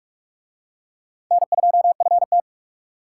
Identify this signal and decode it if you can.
Morse code, heard as a keyed tone, about 40 wpm, 700 Hz. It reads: N2FT